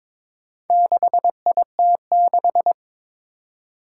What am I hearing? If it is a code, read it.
Morse code, heard as a keyed tone, about 21 wpm, 700 Hz. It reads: 6IT6
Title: Four Challenging Letter-Number Combinations 22wpm